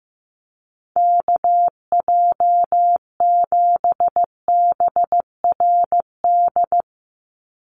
Morse code, heard as a keyed tone, about 15 wpm, 700 Hz. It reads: KJ7BRD
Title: USA Call Signs 15wpm